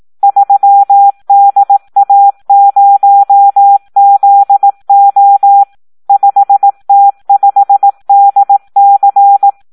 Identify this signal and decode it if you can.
Morse code, heard as a keyed tone, about 18 wpm, 800 Hz. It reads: 3DA0ZO 5T5DC
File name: Audio practica CW / SD 1 18